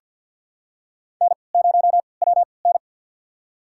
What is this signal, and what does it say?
Morse code, heard as a keyed tone, about 51 wpm, spaced as for 29 wpm, 700 Hz. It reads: N0WN
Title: USA Call Signs - 4 Characters - 3x Character Spacing - 50wpm